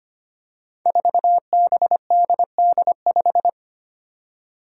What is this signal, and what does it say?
Morse code, heard as a keyed tone, about 25 wpm, 700 Hz. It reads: 4BDD5
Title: Five Challenging Letter-Number Combinations 25wpm